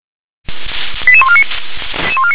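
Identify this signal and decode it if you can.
Five-tone selective call (ZVEI), heard as quick runs of five tones, tones of about 70 ms. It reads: 88149
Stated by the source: ZVEI five-tone selective call received by SDR.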